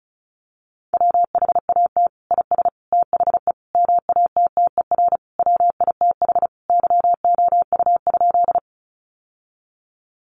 Morse code, heard as a keyed tone, about 34 wpm, 700 Hz. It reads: WHAT IS THE MATTER WITH YOU?
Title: Sentences from Top 700 Words 35wpm